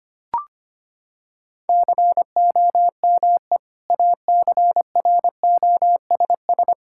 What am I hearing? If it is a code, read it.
Morse code, heard as a keyed tone, about 25 wpm, 700 Hz. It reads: COME ACROSS
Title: Sentences from Top 1000 Words - Review Part 5 - 25wpm